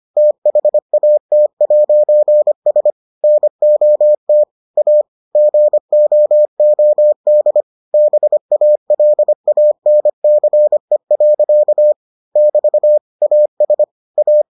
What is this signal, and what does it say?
Morse code, heard as a keyed tone, about 25 wpm, 600 Hz. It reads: THAT'S NOT A GOOD BALANCE. <BT> AS A